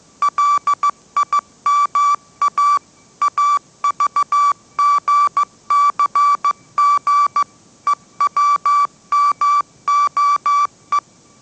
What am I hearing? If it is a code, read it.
Morse code, heard as a keyed tone, about 15 wpm, 1160 Hz. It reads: LIMA AVGCG EWMOE